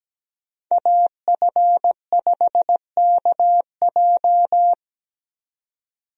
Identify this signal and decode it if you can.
Morse code, heard as a keyed tone, about 17 wpm, 700 Hz. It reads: AF5KJ
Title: USA Call Signs - 5 Characters - 17wpm